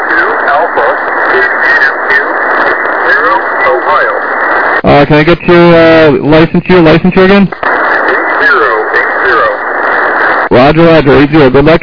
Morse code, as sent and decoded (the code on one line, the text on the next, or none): none